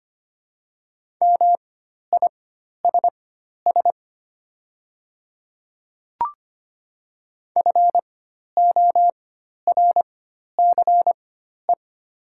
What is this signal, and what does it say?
Morse code, heard as a keyed tone, about 25 wpm, spaced as for 12 wpm, 690 Hz. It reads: MISS FORCE
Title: 4x Character Spacing - Top 500 Words - 25wpm